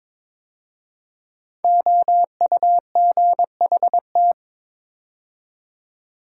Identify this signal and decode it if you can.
Morse code, heard as a keyed tone, about 22 wpm, 700 Hz. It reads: OUGHT
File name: 5 Letter Words - First 3 Letters Given - 22wpm